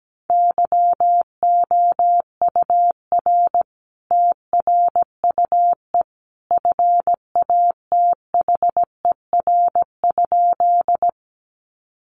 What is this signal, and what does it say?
Morse code, heard as a keyed tone, about 17 wpm, 700 Hz. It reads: YOUR TRUE FATHER?
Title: Sentences from Top 400 Words 17wpm